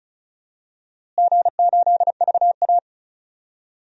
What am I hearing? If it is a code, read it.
Morse code, heard as a keyed tone, about 36 wpm, 700 Hz. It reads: G8VA
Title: DXCC-100 Call Signs - 4 Characters - Repeated 3x - 35wpm